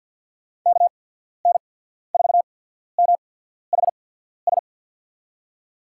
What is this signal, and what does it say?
Morse code, heard as a keyed tone, about 51 wpm, spaced as for 14 wpm, 700 Hz. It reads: KN4MHS
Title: USA Call Signs - 6 Characters - 8x Character Spacing - 50wpm